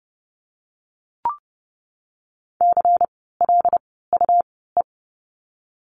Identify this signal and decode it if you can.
Morse code, heard as a keyed tone, about 30 wpm, spaced as for 17 wpm, 700 Hz. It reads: CLUE
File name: Lesson 17f - Letter u - Words 30wpm